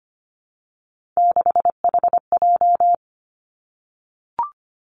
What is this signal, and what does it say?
Morse code, heard as a keyed tone, about 25 wpm, 700 Hz. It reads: 6HJ
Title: Three Challenging Letter-Number Combinations 25wpm